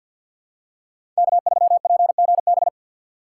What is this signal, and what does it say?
Morse code, heard as a keyed tone, about 51 wpm, 700 Hz. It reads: K3PGB
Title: USA Call Signs - 5 Characters - 4x Character Spacing - 50wpm